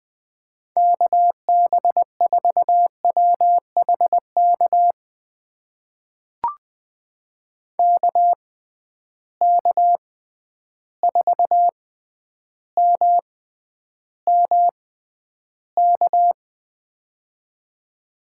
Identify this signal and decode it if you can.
Morse code, heard as a keyed tone, about 20 wpm, spaced as for 10 wpm, 700 Hz. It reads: KB4WHK KK4MMK
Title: USA Call Signs - 6 Characters - 6x Character Spacing - 20wpm